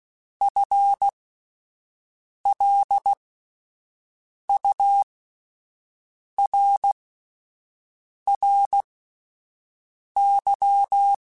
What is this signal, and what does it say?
Morse code, heard as a keyed tone, about 16 wpm, spaced as for 5 wpm, 800 Hz. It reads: FLURRY
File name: Spaced / FL Runs sp mp3s / FL Runs sp @10